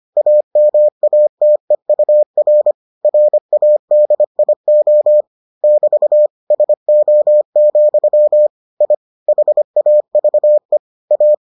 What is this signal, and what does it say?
Morse code, heard as a keyed tone, about 25 wpm, 600 Hz. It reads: AMATEUR RADIO <BT> SO, I HAVE A